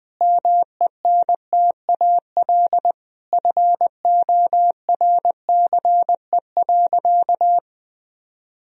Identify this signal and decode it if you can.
Morse code, heard as a keyed tone, about 20 wpm, 700 Hz. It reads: MENTAL FORCE.
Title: Sentences from Top 1000 Words - Review Part 4 - 20wpm